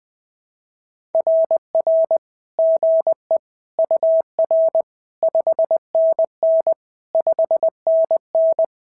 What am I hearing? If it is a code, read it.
Morse code, heard as a keyed tone, about 20 wpm, 690 Hz. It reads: RR GE UR 5NN 5NN
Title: SOTA QSO Practice 20wpm